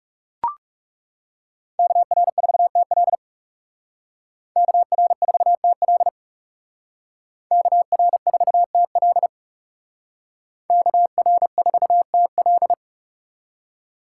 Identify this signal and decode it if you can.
Morse code, heard as a keyed tone, about 38 wpm, 700 Hz. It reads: KR4TL KR4TL KR4TL KR4TL